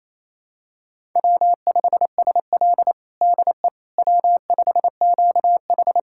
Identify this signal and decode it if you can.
Morse code, heard as a keyed tone, about 27 wpm, 700 Hz. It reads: W5SL DE W5QH
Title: USA Callsign DE Callsign - 4 Characters - 28wpm